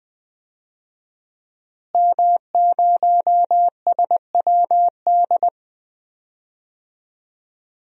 Morse code, heard as a keyed tone, about 20 wpm, 700 Hz. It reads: M0SWD
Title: DXCC-100 Call Signs - 5 Characters - Repeated 2x - 20wpm